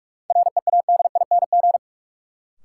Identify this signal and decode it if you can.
Morse code, heard as a keyed tone, about 43 wpm, 700 Hz. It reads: READING
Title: Sentences from Top 600 Words - New Word Drill - 45wpm